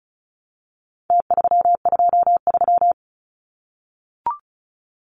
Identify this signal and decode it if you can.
Morse code, heard as a keyed tone, about 34 wpm, 700 Hz. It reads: T323
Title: Numbers - Contest Style - 4-Digit 35wpm